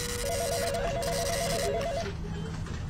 There is no code.